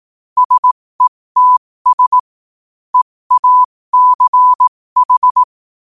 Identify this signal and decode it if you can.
Morse code, heard as a keyed tone, about 18 wpm, spaced as for 15 wpm, 1000 Hz. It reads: SETS EACH
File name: qotdfiles / 15wpm